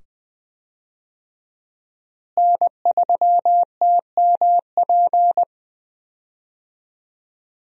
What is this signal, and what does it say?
Morse code, heard as a keyed tone, about 20 wpm, 700 Hz. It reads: N3TMP
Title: USA Call Signs - 5 Characters - Repeated 2x - 20wpm